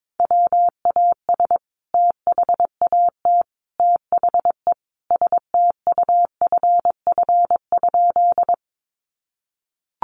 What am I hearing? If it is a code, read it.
Morse code, heard as a keyed tone, about 22 wpm, 700 Hz. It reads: WAS THAT THE STUFF?
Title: Sentences from Top 1400 Words 22wpm